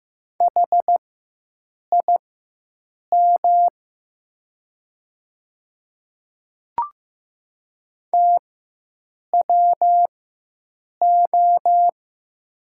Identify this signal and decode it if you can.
Morse code, heard as a keyed tone, about 15 wpm, spaced as for 7 wpm, 700 Hz. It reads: HIM TWO
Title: 4x Character Spacing - Top 100 Words - 15wpm